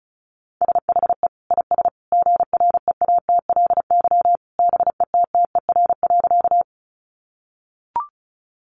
Morse code, heard as a keyed tone, about 34 wpm, 700 Hz. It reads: SHE IS GREATLY BETTER.
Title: Sentences from Top 1400 Words 35wpm